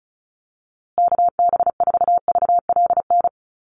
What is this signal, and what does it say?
Morse code, heard as a keyed tone, about 34 wpm, 700 Hz. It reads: KB4VLN
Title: USA Call Signs Repeated 2x 35wpm